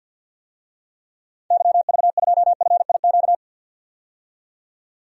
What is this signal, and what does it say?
Morse code, heard as a keyed tone, about 50 wpm, 700 Hz. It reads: YV2FIQ
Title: DXCC-100 Call Signs - 6 Characters - 50wpm